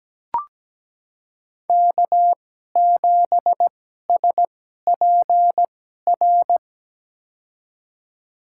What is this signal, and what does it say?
Morse code, heard as a keyed tone, about 17 wpm, spaced as for 12 wpm, 700 Hz. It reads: K7SPR
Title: USA Call Signs - 5 Characters - 2x Character Spacing - 17wpm